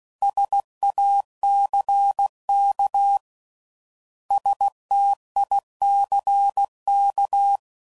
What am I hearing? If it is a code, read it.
Morse code, heard as a keyed tone, about 16 wpm, 800 Hz. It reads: SACK STICK